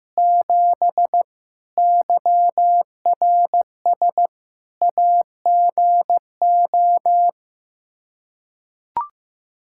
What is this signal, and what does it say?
Morse code, heard as a keyed tone, about 16 wpm, 700 Hz. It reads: 7 YRS AGO
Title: K1USN SST QSO Practice 15wpm